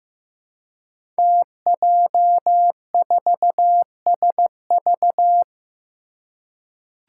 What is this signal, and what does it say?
Morse code, heard as a keyed tone, about 15 wpm, 700 Hz. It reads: TJ4SV
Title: Five Challenging Letter-Number Combinations 15wpm